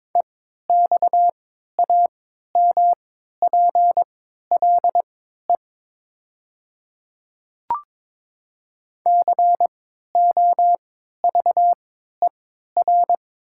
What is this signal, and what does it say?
Morse code, heard as a keyed tone, about 22 wpm, spaced as for 12 wpm, 700 Hz. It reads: EXAMPLE COVER
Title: 3x Character Spacing - Top 300 Words - 22wpm